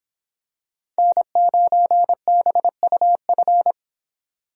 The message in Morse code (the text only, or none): N9BUF